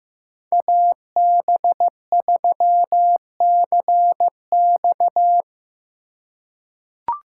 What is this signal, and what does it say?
Morse code, heard as a keyed tone, about 15 wpm, 700 Hz. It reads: AB3CX
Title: USA Call Signs - 5 Characters - 2x Character Spacing - 15wpm